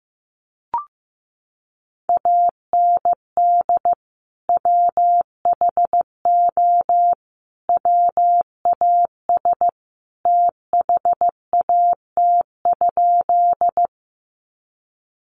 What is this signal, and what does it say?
Morse code, heard as a keyed tone, about 15 wpm, 700 Hz. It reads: AND WHO WAS THAT?
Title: Sentences from Top 100 Words 15wpm